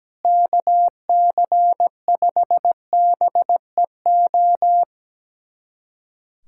Morse code, heard as a keyed tone, about 17 wpm, 700 Hz. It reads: KC5BEO
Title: USA Callsign DE Callsign - 6 Characters - 17wpm